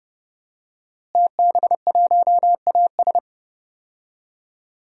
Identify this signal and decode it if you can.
Morse code, heard as a keyed tone, about 30 wpm, 700 Hz. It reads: TB1AS
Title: DXCC-100 Call Signs - 5 Characters - Repeated 2x - 30wpm